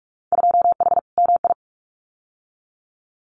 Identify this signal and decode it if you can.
Morse code, heard as a keyed tone, about 46 wpm, 700 Hz. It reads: 2H MI
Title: Winter Field Day Exchanges Encore 45wpm